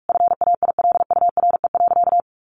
Morse code, heard as a keyed tone, about 44 wpm, 700 Hz. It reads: FAILURE.
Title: Sentences from Top 900 Words 45wpm